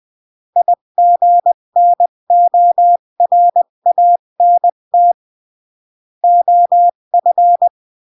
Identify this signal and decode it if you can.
Morse code, heard as a keyed tone, about 20 wpm, 700 Hz. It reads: IGNORANT OF
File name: CW / jodorowsky / 82 commandments 20-15